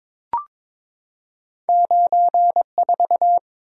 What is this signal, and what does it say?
Morse code, heard as a keyed tone, about 23 wpm, 700 Hz. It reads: 94
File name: practice / 22wpm / Two Letter-Number Combinations - 22wpm